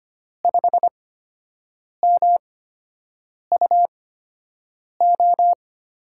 Morse code, heard as a keyed tone, about 25 wpm, spaced as for 7 wpm, 700 Hz. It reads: 5MUO
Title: USA Call Signs - 6 Characters - 8x Character Spacing - 25wpm